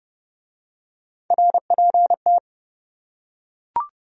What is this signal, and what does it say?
Morse code, heard as a keyed tone, about 28 wpm, 700 Hz. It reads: RPT